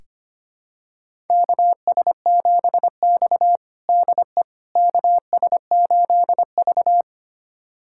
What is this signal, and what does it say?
Morse code, heard as a keyed tone, about 25 wpm, 700 Hz. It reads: KS7X DE KS8V
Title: USA Callsign DE Callsign - 4 Characters - 25wpm